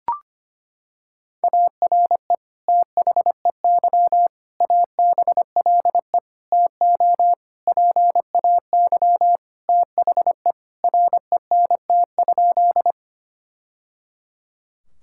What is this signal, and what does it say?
Morse code, heard as a keyed tone, about 25 wpm, 700 Hz. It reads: ARE THEY ABLE TO PAY THE RENT?
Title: Sentences from Top 1000 Words - Review Part 5 - 25wpm